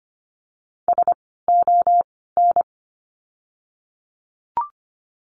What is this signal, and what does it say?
Morse code, heard as a keyed tone, about 25 wpm, spaced as for 16 wpm, 700 Hz. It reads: SON
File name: course / 25:10wpm / 094 Lesson 16f - Letter c - Words 25wpm